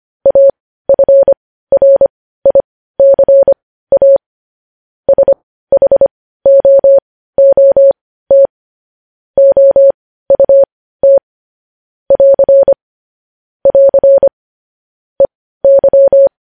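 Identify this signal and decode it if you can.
Morse code, heard as a keyed tone, about 25 wpm, spaced as for 15 wpm, 550 Hz. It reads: AFRICA SHOOT OUT <AR> <AR> EY
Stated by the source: News headlines sent in Morse code at 15 wpm.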